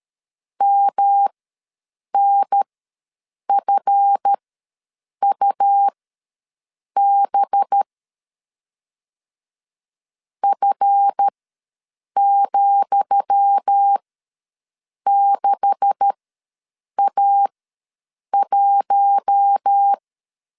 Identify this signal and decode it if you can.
Morse code, heard as a keyed tone, about 13 wpm, spaced as for 7 wpm, 800 Hz. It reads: MNFUB F,6A1
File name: LEZIONI / 10 SIGNS / 10D COMA ws 12wpm 3618